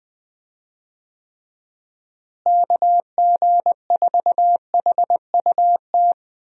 Morse code, heard as a keyed tone, about 21 wpm, 700 Hz. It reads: KG4HUT
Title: USA Call Signs - 6 Characters - Repeated 3x - 20wpm